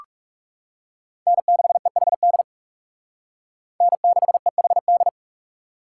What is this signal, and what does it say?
Morse code, heard as a keyed tone, about 37 wpm, 700 Hz. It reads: N6EHD N6EHD
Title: Speed Racing - USA Call Signs 30wpm